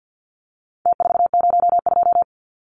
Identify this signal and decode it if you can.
Morse code, heard as a keyed tone, about 50 wpm, 700 Hz. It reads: T402
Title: Numbers - Contest Style - 4-Digit 50wpm